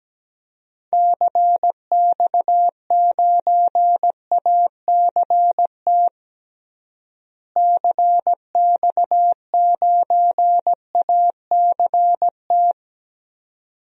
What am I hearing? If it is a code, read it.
Morse code, heard as a keyed tone, about 17 wpm, 700 Hz. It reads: CX9ACT CX9ACT
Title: DXCC-100 Call Signs - 6 Characters - Repeated 2x - 17wpm